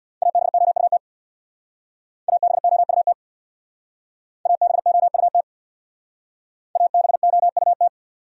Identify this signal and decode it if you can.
Morse code, heard as a keyed tone, about 66 wpm, 700 Hz. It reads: ABOUT ABOUT ABOUT ABOUT